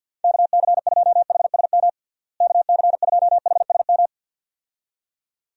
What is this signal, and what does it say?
Morse code, heard as a keyed tone, about 46 wpm, 700 Hz. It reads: KC2HSM KC2HSM
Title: USA Call Signs - 6 Characters - Repeated 2x - 50wpm